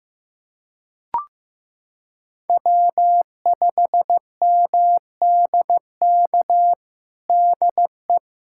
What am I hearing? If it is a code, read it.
Morse code, heard as a keyed tone, about 15 wpm, 700 Hz. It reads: W5MDK DE
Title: USA Callsign DE Callsign - 5 Characters - 15wpm